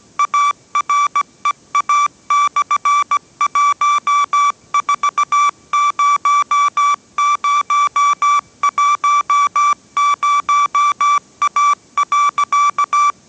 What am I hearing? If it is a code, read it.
Morse code, heard as a keyed tone, about 17 wpm, 1160 Hz. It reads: AREA/140010A.
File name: 18wpm